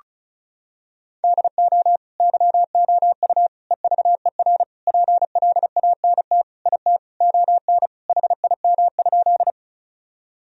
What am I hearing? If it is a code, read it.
Morse code, heard as a keyed tone, about 36 wpm, 700 Hz. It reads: DO YOU EVER PLANT IT ON HIM?
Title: Sentences from Top 1000 Words - Review Part 1 - 35wpm